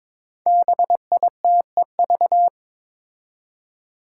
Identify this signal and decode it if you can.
Morse code, heard as a keyed tone, about 22 wpm, 700 Hz. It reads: BITEV
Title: Five Challenging Letter-Number Combinations 22wpm